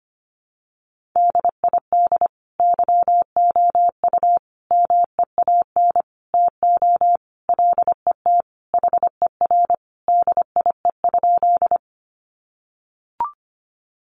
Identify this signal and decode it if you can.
Morse code, heard as a keyed tone, about 25 wpm, 700 Hz. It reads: DID YOU MEAN TO LET HER DIE?
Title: Sentences from Top 1300 Words 25wpm